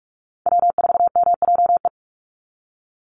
Morse code, heard as a keyed tone, about 44 wpm, 700 Hz. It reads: W4MJE
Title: Call Signs Repeated 2x - 45wpm